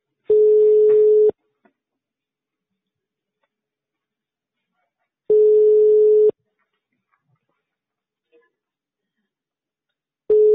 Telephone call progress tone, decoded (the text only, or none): ringback tone